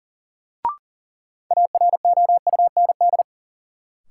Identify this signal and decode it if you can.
Morse code, heard as a keyed tone, about 40 wpm, 700 Hz. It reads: AROUND
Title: Sentences from Top 600 Words - New Word Drill - 40wpm